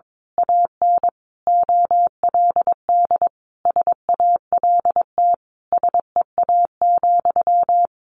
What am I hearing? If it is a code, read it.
Morse code, heard as a keyed tone, about 22 wpm, 700 Hz. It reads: AN OLD SALT SEA,
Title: A Sound Of Thunder 22wpm